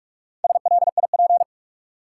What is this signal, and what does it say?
Morse code, heard as a keyed tone, about 47 wpm, 700 Hz. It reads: SLIP
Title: Four Letter Words - 45wpm